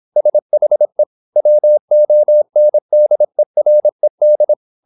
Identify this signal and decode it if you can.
Morse code, heard as a keyed tone, about 26 wpm, 600 Hz. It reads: SHE WONDERED
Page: Never 26wpm | CW med Gnister